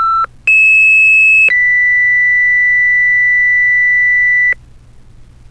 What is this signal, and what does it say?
Two-tone page, heard as a one-second tone, then a three-second tone, about 2600 then 2000 Hz, with noise. It is heard from 0.5 to 4.5 s.
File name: ringtones / fire-pager